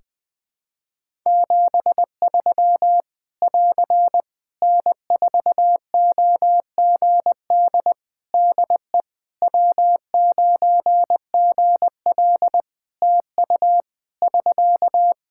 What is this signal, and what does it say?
Morse code, heard as a keyed tone, about 20 wpm, 700 Hz. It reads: 73 <AR> N4OGD DE W9GL TU <SK>
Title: K1USN SST QSO Practice 20wpm